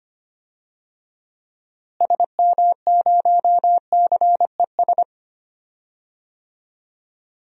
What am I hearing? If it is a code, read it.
Morse code, heard as a keyed tone, about 25 wpm, 700 Hz. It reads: SM0CES
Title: DXCC-100 Call Signs - 6 Characters - Repeated 2x - 25wpm